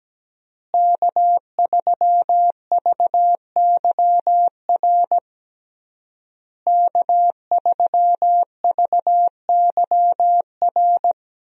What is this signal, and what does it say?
Morse code, heard as a keyed tone, about 17 wpm, 700 Hz. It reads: K3VYR K3VYR
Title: USA Call Signs - 5 Characters - Repeated 3x - 17wpm